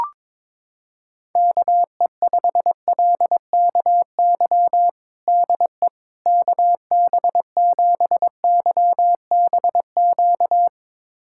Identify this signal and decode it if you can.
Morse code, heard as a keyed tone, about 22 wpm, 700 Hz. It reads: KE5LKY DE KB7YBQ
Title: USA Callsign DE Callsign - 6 Characters - 22wpm